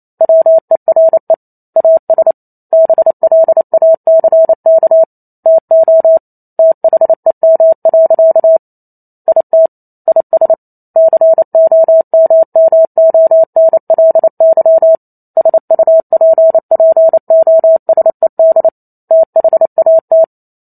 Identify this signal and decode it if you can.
Morse code, heard as a keyed tone, about 28 wpm, 650 Hz. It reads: WERE AS BLACK TO THEM. IT IS COMMONLY SUPPOSED THAT